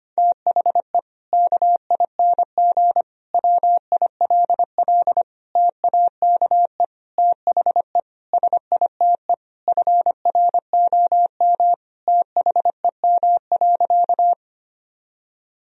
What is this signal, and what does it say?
Morse code, heard as a keyed tone, about 25 wpm, 700 Hz. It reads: THE KING WILL TAKE THE SITE FROM THEM.
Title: Sentences from Top 1000 Words - Review Part 4 - 25wpm